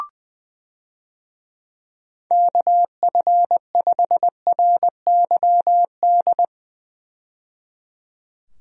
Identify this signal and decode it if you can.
Morse code, heard as a keyed tone, about 21 wpm, 700 Hz. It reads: KF5RYD
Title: USA Call Signs - 6 Characters - 20wpm